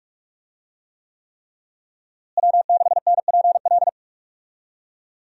Morse code, heard as a keyed tone, about 47 wpm, 700 Hz. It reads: W6NPL
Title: USA Call Signs - 5 Characters - Repeated 3x - 45wpm